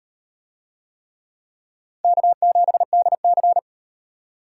Morse code, heard as a keyed tone, about 38 wpm, 700 Hz. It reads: K7DC